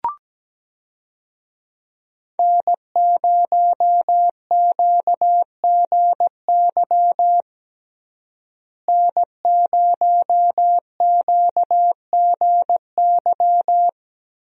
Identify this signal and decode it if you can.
Morse code, heard as a keyed tone, about 17 wpm, 700 Hz. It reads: N0QGY N0QGY
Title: USA Call Signs - 5 Characters - Repeated 2x - 17wpm